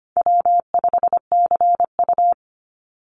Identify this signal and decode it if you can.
Morse code, heard as a keyed tone, about 25 wpm, 700 Hz. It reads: W5CU
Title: Easy Call Signs Repeated 2x - 25wpm